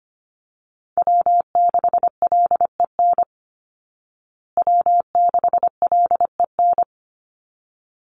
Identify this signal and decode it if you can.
Morse code, heard as a keyed tone, about 25 wpm, 700 Hz. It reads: W6LEN W6LEN
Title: Call Signs Repeated 3x - 25wpm